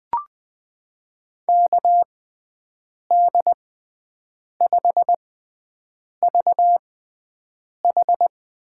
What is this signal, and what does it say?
Morse code, heard as a keyed tone, about 20 wpm, spaced as for 7 wpm, 700 Hz. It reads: KD5VH
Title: USA Call Signs - 6 Characters - 6x Character Spacing - 20wpm